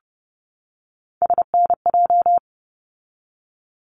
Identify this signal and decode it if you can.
Morse code, heard as a keyed tone, about 30 wpm, 700 Hz. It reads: SNJ
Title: ARRL Sections 30wpm